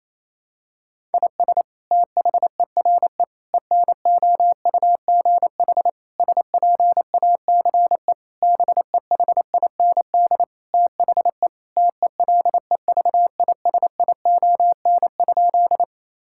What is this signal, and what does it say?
Morse code, heard as a keyed tone, about 28 wpm, 700 Hz. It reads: IS THERE ENOUGH SPACE BEHIND THE TELEVISION?
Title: Sentences from Top 1000 Words - Review Part 5 - 28wpm